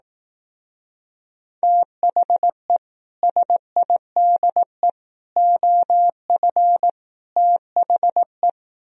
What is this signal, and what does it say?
Morse code, heard as a keyed tone, about 18 wpm, 700 Hz. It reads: THE SIDE OF THE
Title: Speed-Racing - Sets of 4 Words 15wpm